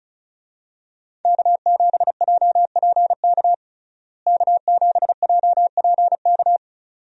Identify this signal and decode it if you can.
Morse code, heard as a keyed tone, about 36 wpm, 700 Hz. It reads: K7JPK K7JPK
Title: USA Call Signs - 5 Characters - Repeated 2x - 35wpm